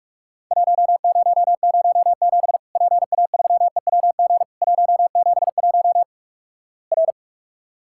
Morse code, heard as a keyed tone, about 48 wpm, 700 Hz. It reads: 1007 PA3EWG 171 R
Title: Worked All Europe DX Contest - QTC Exchange - 45wpm